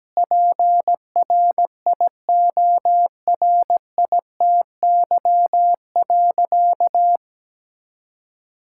Morse code, heard as a keyed tone, about 17 wpm, 700 Hz. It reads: PRIORITY.
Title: Sentences from Top 1600 Words 17wpm